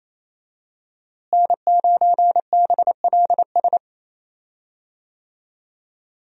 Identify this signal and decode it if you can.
Morse code, heard as a keyed tone, about 28 wpm, 700 Hz. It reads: N9BLS
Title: USA Call Signs - 5 Characters - Repeated 2x - 28wpm